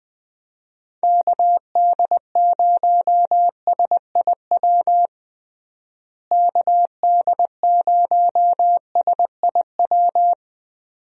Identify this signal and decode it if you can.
Morse code, heard as a keyed tone, about 21 wpm, 700 Hz. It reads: KD0SIW KD0SIW
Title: USA Call Signs - 6 Characters - Repeated 2x - 20wpm